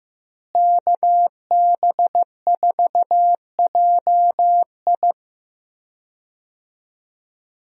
Morse code, heard as a keyed tone, about 15 wpm, 700 Hz. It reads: KB4JI
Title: USA Call Signs - 5 Characters - Repeated 2x - 15wpm